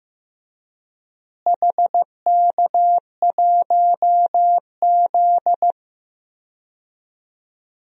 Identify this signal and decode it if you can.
Morse code, heard as a keyed tone, about 15 wpm, 700 Hz. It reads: HK1Z
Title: DXCC-100 Call Signs - 4 Characters - Repeated 3x - 15wpm